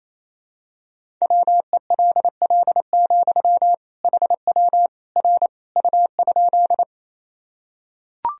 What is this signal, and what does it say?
Morse code, heard as a keyed tone, about 27 wpm, 700 Hz. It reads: WELL, HW R U?
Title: Sentences from Top 200 Words and Abbreviations 28wpm